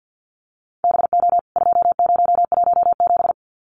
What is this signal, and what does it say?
Morse code, heard as a keyed tone, about 53 wpm, 700 Hz. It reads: BO 2017
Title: CWT Exchange 50wpm